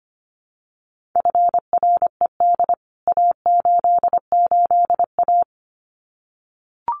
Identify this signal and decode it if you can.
Morse code, heard as a keyed tone, about 25 wpm, 700 Hz. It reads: FRED A88A
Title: CWT Exchange 25wpm